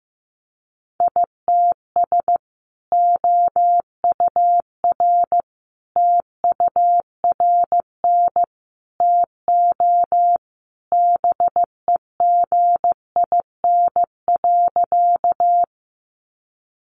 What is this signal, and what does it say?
Morse code, heard as a keyed tone, about 15 wpm, 700 Hz. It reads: ITS OUR TURN TO BEGIN.
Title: Sentences from Top 600 Words 15wpm